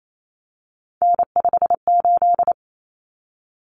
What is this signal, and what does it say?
Morse code, heard as a keyed tone, about 28 wpm, 700 Hz. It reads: N58
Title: Common Cut Numbers - 3-Digit 28wpm